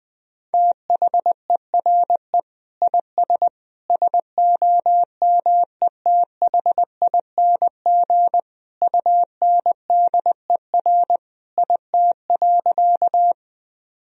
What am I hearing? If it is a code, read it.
Morse code, heard as a keyed tone, about 19 wpm, 700 Hz. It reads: THERE IS SOMETHING UNDER IT.
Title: Sentences from Top 1000 Words - Review Part 1 - 20wpm